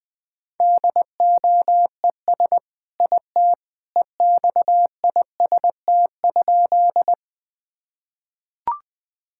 Morse code, heard as a keyed tone, about 21 wpm, 700 Hz. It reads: DOES IT EXIST?
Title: Sentences from Top 1000 Words - Review Part 5 - 20wpm